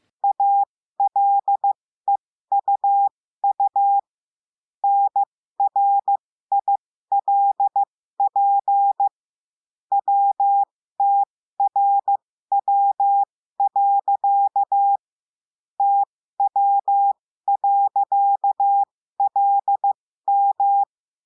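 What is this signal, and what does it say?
Morse code, heard as a keyed tone, about 15 wpm, spaced as for 13 wpm, 800 Hz. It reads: ALEUU NRILP WTRW. TW.LM
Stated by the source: Morse lesson adding the period, sending practice five-character groups.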